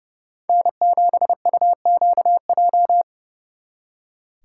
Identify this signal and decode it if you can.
Morse code, heard as a keyed tone, about 38 wpm, 700 Hz. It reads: N7UQJ